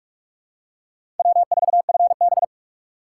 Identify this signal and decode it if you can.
Morse code, heard as a keyed tone, about 47 wpm, 700 Hz. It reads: W4FB
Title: USA Call Signs - 4x Character Spacing - 45wpm